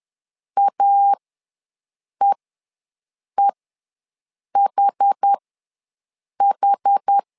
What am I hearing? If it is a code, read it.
Morse code, heard as a keyed tone, about 10 wpm, spaced as for 6 wpm, 800 Hz. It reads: AEEHH
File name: LEZIONI / 2 UHDN / 2 UH ws 10wpm 2029